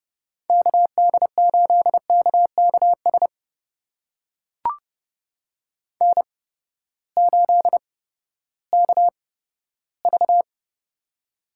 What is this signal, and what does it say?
Morse code, heard as a keyed tone, about 30 wpm, spaced as for 15 wpm, 700 Hz. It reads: KD8KKS N8KV
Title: USA Call Signs - 8x Character Spacing - 30wpm